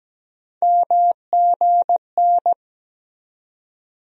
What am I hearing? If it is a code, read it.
Morse code, heard as a keyed tone, about 17 wpm, 700 Hz. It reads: MGN
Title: USA Call Sign Postfixes - 3 Letters 17wpm